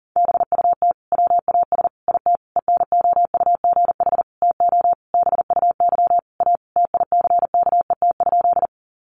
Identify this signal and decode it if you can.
Morse code, heard as a keyed tone, about 40 wpm, 700 Hz. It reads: BUT WAS IT ENOUGH TO BUY A TICKET?
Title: Sentences from Top 1400 Words 40wpm